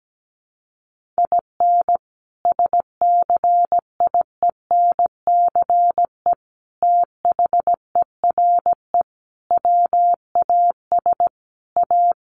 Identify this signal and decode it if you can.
Morse code, heard as a keyed tone, about 17 wpm, 700 Hz. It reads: IN SCIENCE THERE WAS A
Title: Sentences from Top 1400 Words 17wpm